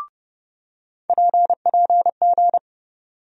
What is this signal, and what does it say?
Morse code, heard as a keyed tone, about 30 wpm, 700 Hz. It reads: PPG
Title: USA Call Sign Postfixes - 3 Letters 30wpm